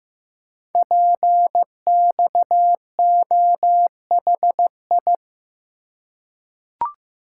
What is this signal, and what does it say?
Morse code, heard as a keyed tone, about 15 wpm, 700 Hz. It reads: PXOHI
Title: Code Groups - 5 Letters - 15wpm